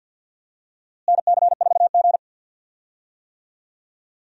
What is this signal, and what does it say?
Morse code, heard as a keyed tone, about 46 wpm, 700 Hz. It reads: NC4G
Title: USA Call Signs - 4 Characters - Repeated 2x - 50wpm